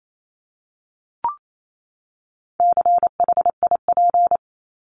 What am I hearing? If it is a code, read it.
Morse code, heard as a keyed tone, about 28 wpm, 700 Hz. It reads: CHIP
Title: Top 1751-2000 Words 28wpm